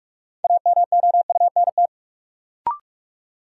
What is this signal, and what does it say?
Morse code, heard as a keyed tone, about 46 wpm, 700 Hz. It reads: AMOUNT
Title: Sentences from Top 700 Words - New Word Drill - 45wpm